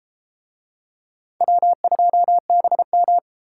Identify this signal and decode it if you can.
Morse code, heard as a keyed tone, about 30 wpm, 700 Hz. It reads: W2BM